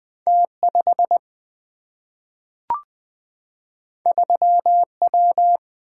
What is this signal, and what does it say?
Morse code, heard as a keyed tone, about 20 wpm, 700 Hz. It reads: T 5 3 W
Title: Single-Letter Single-Number 20wpm